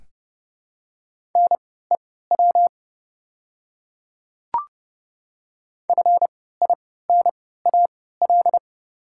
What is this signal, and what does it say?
Morse code, heard as a keyed tone, about 30 wpm, spaced as for 17 wpm, 700 Hz. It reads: NEW FINAL
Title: 3x Character Spacing - Top 500 Words - 30wpm